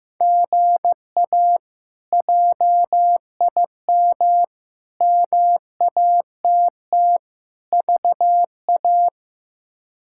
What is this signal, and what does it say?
Morse code, heard as a keyed tone, about 15 wpm, 700 Hz. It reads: GA JIM MATT VA
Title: K1USN SST QSO Practice - 15wpm